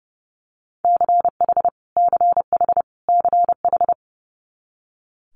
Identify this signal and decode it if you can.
Morse code, heard as a keyed tone, about 30 wpm, 700 Hz. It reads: CH CH CH
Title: Multi-letter Phonemes Repeated 3x 30wpm